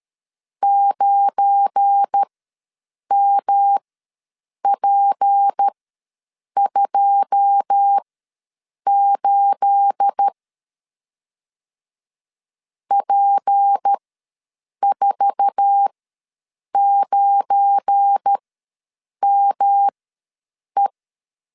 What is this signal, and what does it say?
Morse code, heard as a keyed tone, about 13 wpm, spaced as for 7 wpm, 800 Hz. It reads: 9MP28 P49ME
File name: LEZIONI / 9 28 ws 12wpm 3354